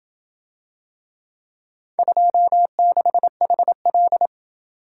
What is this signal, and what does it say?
Morse code, heard as a keyed tone, about 28 wpm, 700 Hz. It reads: 26HL